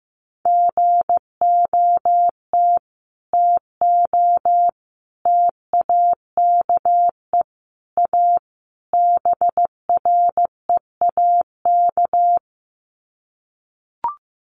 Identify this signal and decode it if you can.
Morse code, heard as a keyed tone, about 15 wpm, 700 Hz. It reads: GOT TO TAKE A BREAK
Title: Sets of 5 Words Encore 15wpm